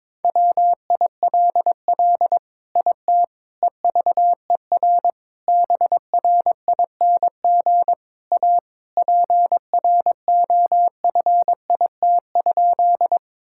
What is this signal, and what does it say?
Morse code, heard as a keyed tone, about 22 wpm, 700 Hz. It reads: WILL IT EVER BRING A PROFIT?
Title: Sentences from Top 1000 Words - Review Part 5 - 22wpm